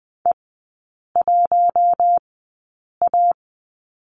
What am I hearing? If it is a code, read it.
Morse code, heard as a keyed tone, about 20 wpm, spaced as for 8 wpm, 700 Hz. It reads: E1A
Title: 053 Lesson 10e - Letter r - Three Characters 20wpm